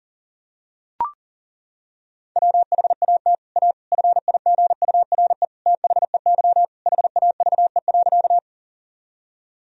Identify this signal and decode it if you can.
Morse code, heard as a keyed tone, about 40 wpm, 700 Hz. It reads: WHAT A FIGURE THEY HAVE.
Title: Sentences from Top 1000 Words - Review Part 2 - 40wpm